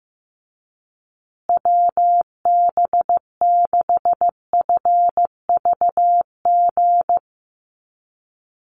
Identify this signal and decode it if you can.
Morse code, heard as a keyed tone, about 15 wpm, 700 Hz. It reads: WB6FVG
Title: USA Call Signs Repeated 3x 15wpm